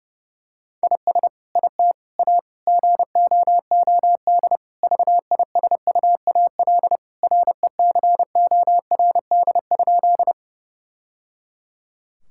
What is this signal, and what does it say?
Morse code, heard as a keyed tone, about 30 wpm, 700 Hz. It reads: IS IT A GOOD VISUAL RECORD?
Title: Sentences from Top 1000 Words - Review Part 6 - 30wpm